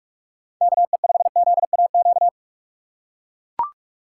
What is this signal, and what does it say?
Morse code, heard as a keyed tone, about 44 wpm, 700 Hz. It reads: KE5ZAQ
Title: USA Call Signs - 6x Character Spacing - 45wpm